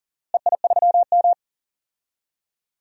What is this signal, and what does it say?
Morse code, heard as a keyed tone, about 38 wpm, 700 Hz. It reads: EI3M
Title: DXCC-100 Call Signs - 4 Characters - Repeated 3x - 40wpm